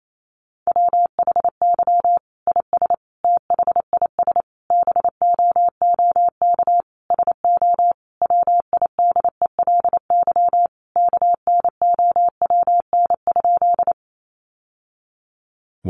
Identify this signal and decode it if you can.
Morse code, heard as a keyed tone, about 28 wpm, 700 Hz. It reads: WHY IS THIS BOOK SO WIDELY KNOWN?
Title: Sentences from Top 1200 Words 28wpm